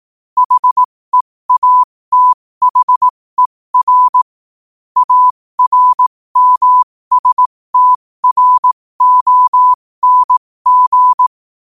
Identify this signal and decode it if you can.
Morse code, heard as a keyed tone, about 18 wpm, spaced as for 15 wpm, 1000 Hz. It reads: HEATHER ARMSTRONG
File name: qotdfiles / 15wpm